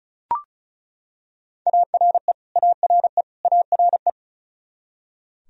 Morse code, heard as a keyed tone, about 36 wpm, 700 Hz. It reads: ARE ARE ARE
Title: Top 100 Words Repeated 35wpm